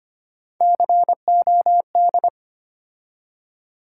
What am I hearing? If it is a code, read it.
Morse code, heard as a keyed tone, about 25 wpm, 700 Hz. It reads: COD
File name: practice / 25wpm / Difficult Words with Hints - 25wpm